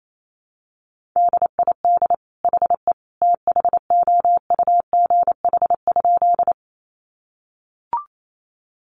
Morse code, heard as a keyed tone, about 28 wpm, 700 Hz. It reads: DID HE THOUGH?
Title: Sentences from Top 400 Words 28wpm